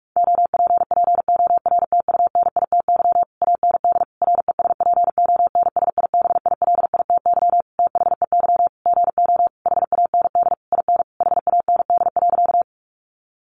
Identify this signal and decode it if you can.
Morse code, heard as a keyed tone, about 44 wpm, 700 Hz. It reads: OPPORTUNITY AND RESPONSIBILITY THEY GO HAND IN HAND.
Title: Sentences from Top 1200 Words 45wpm